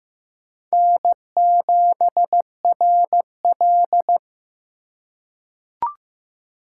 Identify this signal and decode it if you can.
Morse code, heard as a keyed tone, about 15 wpm, 700 Hz. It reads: N7RL
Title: USA Call Signs - 4 Characters - 4x Character Spacing - 15wpm